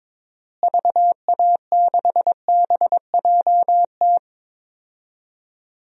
Morse code, heard as a keyed tone, about 22 wpm, 700 Hz. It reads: VA6BJT
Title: DXCC-100 Call Signs - 6 Characters - Repeated 2x - 22wpm